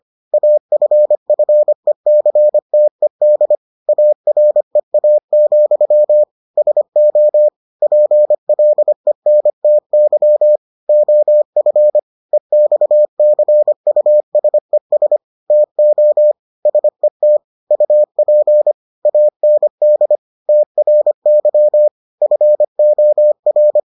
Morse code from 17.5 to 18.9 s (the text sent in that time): UP